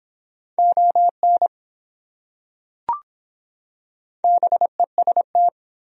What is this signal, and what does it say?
Morse code, heard as a keyed tone, about 23 wpm, 700 Hz. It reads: ON BEST